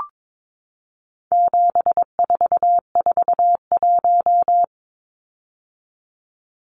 Morse code, heard as a keyed tone, about 22 wpm, 700 Hz. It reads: 7441
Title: Numbers - 4-Digit 22wpm